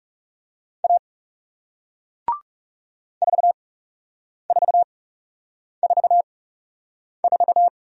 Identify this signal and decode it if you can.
Morse code, heard as a keyed tone, about 40 wpm, spaced as for 8 wpm, 700 Hz. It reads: A 4444